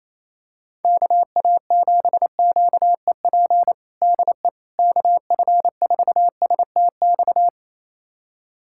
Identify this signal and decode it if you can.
Morse code, heard as a keyed tone, about 28 wpm, 700 Hz. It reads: KA7QEP DE KF4STX
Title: USA Callsign DE Callsign 28wpm